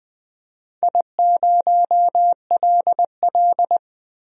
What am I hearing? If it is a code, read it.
Morse code, heard as a keyed tone, about 20 wpm, 700 Hz. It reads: I0LL
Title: DXCC-100 Call Signs - 4 Characters - Repeated 2x - 20wpm